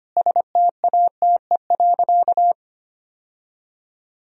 Morse code, heard as a keyed tone, about 25 wpm, 700 Hz. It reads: STATE.
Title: Sentences from Top 1000 Words - Review Part 2 - 25wpm